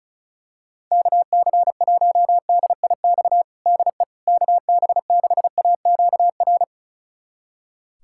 Morse code, heard as a keyed tone, about 36 wpm, 700 Hz. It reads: KC1DIX DE KB6AQR
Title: USA Callsign DE Callsign 35wpm